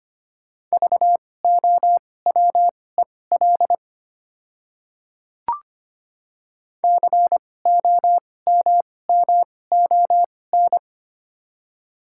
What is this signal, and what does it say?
Morse code, heard as a keyed tone, about 25 wpm, spaced as for 18 wpm, 700 Hz. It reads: VOWEL COMMON
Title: 2x Character Spacing - Top 500 Words - 25wpm